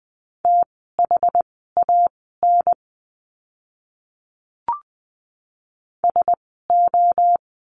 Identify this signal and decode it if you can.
Morse code, heard as a keyed tone, about 20 wpm, spaced as for 15 wpm, 700 Hz. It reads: THAN SO
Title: 230 Lesson 37f - Letter x - Words 20wpm